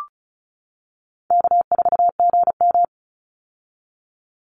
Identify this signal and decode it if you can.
Morse code, heard as a keyed tone, about 34 wpm, 700 Hz. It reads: K4GM
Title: Call Signs - 35wpm